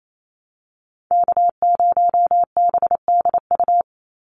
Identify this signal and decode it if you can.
Morse code, heard as a keyed tone, about 28 wpm, 700 Hz. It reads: K0BDU
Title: USA Call Signs 28wpm